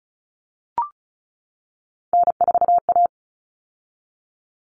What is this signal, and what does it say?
Morse code, heard as a keyed tone, about 34 wpm, 700 Hz. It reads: N4A
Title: Common Cut Numbers - 3-Digit 35wpm